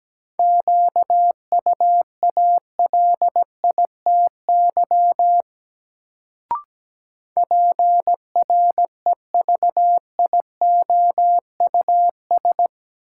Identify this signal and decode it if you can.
Morse code, heard as a keyed tone, about 17 wpm, 700 Hz. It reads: QUALITY PREVIOUS
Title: Sentences from Top 1500 Words - New Word Drill Alt - 17wpm